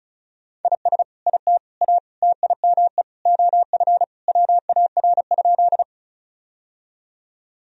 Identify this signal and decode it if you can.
Morse code, heard as a keyed tone, about 36 wpm, 700 Hz. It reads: IS IT A TIME OF WAR?
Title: Sentences from Top 1000 Words - Review Part 2 - 35wpm